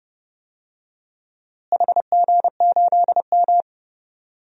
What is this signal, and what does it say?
Morse code, heard as a keyed tone, about 31 wpm, 700 Hz. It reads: HG8M
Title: DXCC-100 Call Signs - 4 Characters - Repeated 3x - 30wpm